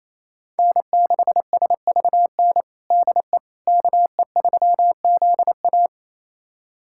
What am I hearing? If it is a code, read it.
Morse code, heard as a keyed tone, about 28 wpm, 700 Hz. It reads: N6SVN DE KE3ZA
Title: USA Callsign DE Callsign - 5 Characters - 28wpm